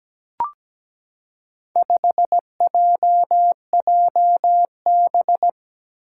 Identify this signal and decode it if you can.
Morse code, heard as a keyed tone, about 17 wpm, 700 Hz. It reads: 5JJB
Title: Four Challenging Letter-Number Combinations 17wpm